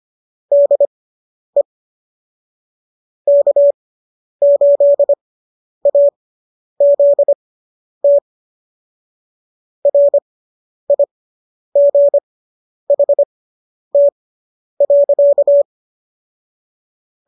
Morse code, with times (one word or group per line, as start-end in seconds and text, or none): 0.5-1.6 DE
3.3-8.2 K8AZT
9.9-15.6 RIGHT.